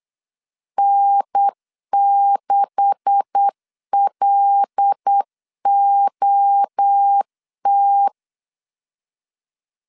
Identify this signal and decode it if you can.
Morse code, heard as a keyed tone, about 8 wpm, 800 Hz. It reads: N6LOT
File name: LEZIONI / 10 SIGNS / 10D COMA 8wpm 3749